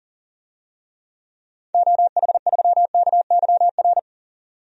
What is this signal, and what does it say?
Morse code, heard as a keyed tone, about 40 wpm, 700 Hz. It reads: OH3KYR
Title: DXCC-100 Call Signs - 6 Characters - 40wpm